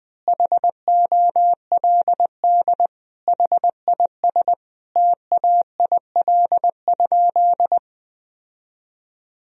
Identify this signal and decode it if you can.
Morse code, heard as a keyed tone, about 21 wpm, 700 Hz. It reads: HOLD HIS TAIL?
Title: Sentences from Top 1000 Words - Review Part 3 - 20wpm